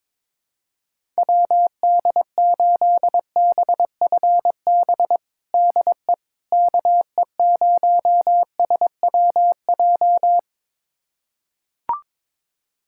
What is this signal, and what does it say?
Morse code, heard as a keyed tone, about 22 wpm, 700 Hz. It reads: WD8BFB DE KE0SWJ
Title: USA Callsign DE Callsign - 6 Characters - 22wpm